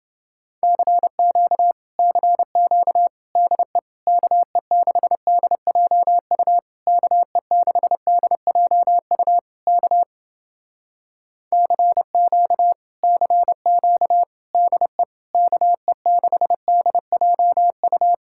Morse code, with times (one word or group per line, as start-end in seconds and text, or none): 0.6-1.7 CQ
2.0-3.1 CQ
3.4-3.8 DE
4.1-6.6 KE6DJU
6.9-9.4 KE6DJU
9.7-10.0 K
11.5-12.7 CQ
13.0-14.2 CQ
14.5-15.0 DE
15.3-18.1 KE6DJU